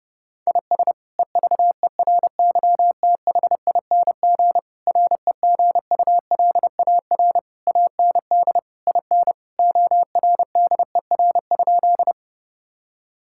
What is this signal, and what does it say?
Morse code, heard as a keyed tone, about 30 wpm, 700 Hz. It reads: IS EVERYTHING REGULAR AND IN ORDER?
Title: Sentences from Top 1000 Words - Review Part 6 - 30wpm